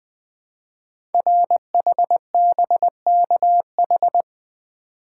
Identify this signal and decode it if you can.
Morse code, heard as a keyed tone, about 20 wpm, 700 Hz. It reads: RHBKH
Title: Code Groups - 5 Letters - 20wpm